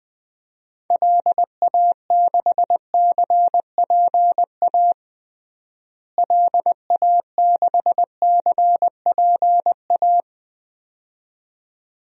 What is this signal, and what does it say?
Morse code, heard as a keyed tone, about 20 wpm, 700 Hz. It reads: LA6CPA LA6CPA
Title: DXCC-100 Call Signs - 6 Characters - Repeated 2x - 20wpm